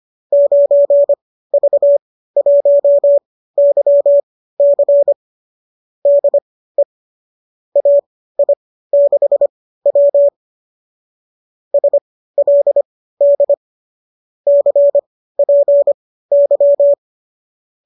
Morse code, heard as a keyed tone, about 25 wpm, spaced as for 15 wpm, 590 Hz. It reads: 9V1YC DE AI6W SLD CPY
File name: qso207 15